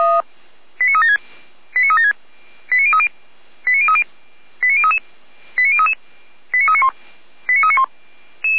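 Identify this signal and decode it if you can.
Five-tone selective call (ZVEI), heard as quick runs of five tones, tones of about 70 ms. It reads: 89376 89376 89030 89030 89033 89033 89391 89391
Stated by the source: Five-tone ZVEI selective call 89187, received by SDR.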